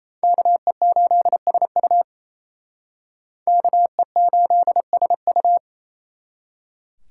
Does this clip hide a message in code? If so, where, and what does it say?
Yes: Morse code from 0.2 to 5.6 s, heard as a keyed tone, about 38 wpm, 700 Hz. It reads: KE8SU KE8SU